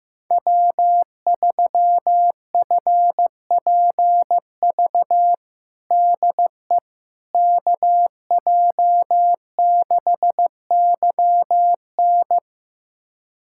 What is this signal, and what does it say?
Morse code, heard as a keyed tone, about 16 wpm, 700 Hz. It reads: W3FPV DE KJ6YN
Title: USA Callsign DE Callsign - 5 Characters - 15wpm